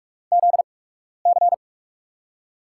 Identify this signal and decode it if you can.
Morse code, heard as a keyed tone, about 44 wpm, spaced as for 12 wpm, 690 Hz. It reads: ZC
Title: USA Call Signs - 4 Characters - 8x Character Spacing - 45wpm